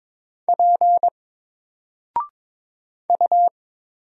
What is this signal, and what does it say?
Morse code, heard as a keyed tone, about 23 wpm, 700 Hz. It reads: P U
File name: Single-Letter-Number-22wpm